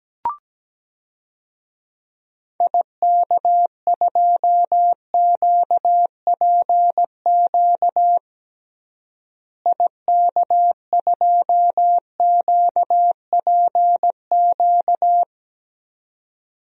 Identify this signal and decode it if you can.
Morse code, heard as a keyed tone, about 17 wpm, 700 Hz. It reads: IK2QPQ IK2QPQ
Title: DXCC-100 Call Signs - 6 Characters - Repeated 2x - 17wpm